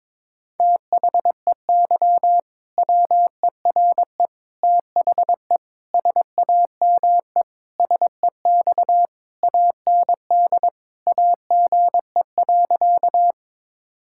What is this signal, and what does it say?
Morse code, heard as a keyed tone, about 22 wpm, 700 Hz. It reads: THEY WERE THE SAME SEX AND AGE.
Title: Sentences from Top 1600 Words 22wpm